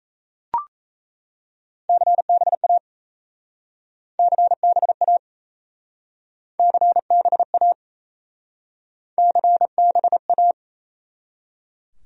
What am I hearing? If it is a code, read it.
Morse code, heard as a keyed tone, about 37 wpm, 700 Hz. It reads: CBA CBA CBA CBA